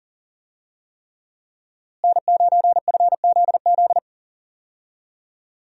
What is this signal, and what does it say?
Morse code, heard as a keyed tone, about 38 wpm, 700 Hz. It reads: N9FZZ
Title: USA Call Signs - 5 Characters - Repeated 3x - 40wpm